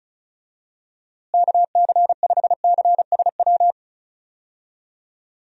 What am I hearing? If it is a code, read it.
Morse code, heard as a keyed tone, about 34 wpm, 700 Hz. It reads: KC5CSW
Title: USA Call Signs - 6 Characters - Repeated 3x - 35wpm